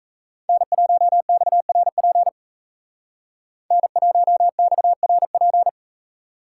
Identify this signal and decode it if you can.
Morse code, heard as a keyed tone, about 37 wpm, 700 Hz. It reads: N1XRP N1XRP